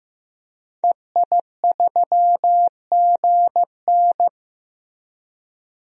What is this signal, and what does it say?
Morse code, heard as a keyed tone, about 15 wpm, 700 Hz. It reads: EI3GN
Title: DXCC-100 Call Signs - 5 Characters - 15wpm